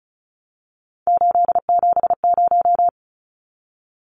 Morse code, heard as a keyed tone, about 34 wpm, 700 Hz. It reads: 870
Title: Numbers - 3-Digit 35wpm